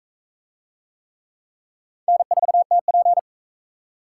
Morse code, heard as a keyed tone, about 39 wpm, 700 Hz. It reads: N4TP